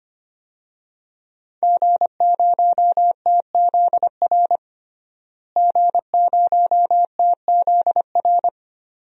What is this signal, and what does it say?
Morse code, heard as a keyed tone, about 25 wpm, 700 Hz. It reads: G0TZR G0TZR
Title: DXCC-100 Call Signs - 5 Characters - Repeated 2x - 25wpm